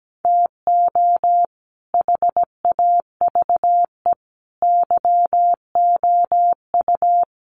Morse code, heard as a keyed tone, about 17 wpm, 700 Hz. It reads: TO HAVE YOU
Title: Sets of 5 Words 17wpm